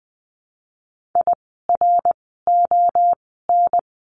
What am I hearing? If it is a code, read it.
Morse code, heard as a keyed tone, about 20 wpm, 700 Hz. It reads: I R O N
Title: 054 Lesson 10f - Letter r - Words 20wpm